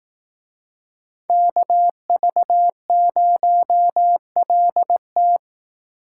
Morse code, heard as a keyed tone, about 18 wpm, 700 Hz. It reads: KV0LT
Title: Speed Racing - USA Call Signs 15wpm